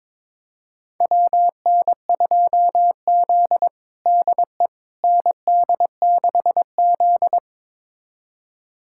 Morse code, heard as a keyed tone, about 22 wpm, 700 Hz. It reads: WN2Z DE ND6Z
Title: USA Callsign DE Callsign - 4 Characters - 22wpm